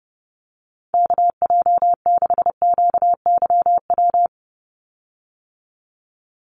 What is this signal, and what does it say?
Morse code, heard as a keyed tone, about 30 wpm, 700 Hz. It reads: KJ6QYW
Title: USA Call Signs 30wpm